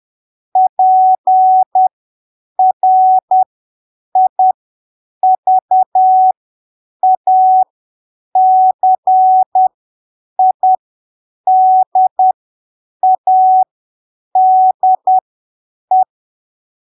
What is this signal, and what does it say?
Morse code, heard as a keyed tone, about 10 wpm, spaced as for 7 wpm, 750 Hz. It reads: PRIVACIDADE
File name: arquivos / texto / texto 30 10ppm